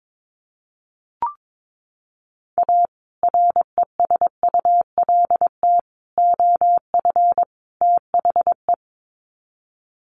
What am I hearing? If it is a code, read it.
Morse code, heard as a keyed tone, about 22 wpm, 700 Hz. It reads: A RESULT OF THE
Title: Sets of 4 Words 22wpm